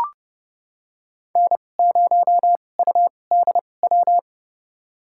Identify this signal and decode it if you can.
Morse code, heard as a keyed tone, about 31 wpm, spaced as for 22 wpm, 700 Hz. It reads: N0UDW
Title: USA Call Signs - 5 Characters - 2x Character Spacing - 30wpm